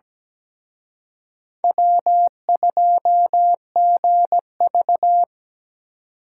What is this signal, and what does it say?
Morse code, heard as a keyed tone, about 20 wpm, 700 Hz. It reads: W2GV
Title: Speed Racing - USA Call Signs 17wpm